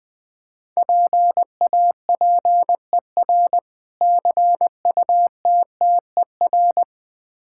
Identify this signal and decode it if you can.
Morse code, heard as a keyed tone, about 21 wpm, 690 Hz. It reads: PAPER CUTTER
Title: Difficult Words with Hints - 20wpm